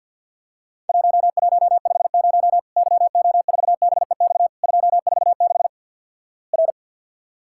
Morse code, heard as a keyed tone, about 50 wpm, 700 Hz. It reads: 1150 YO4BEX 246 R
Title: Worked All Europe DX Contest - QTC Exchange - 50wpm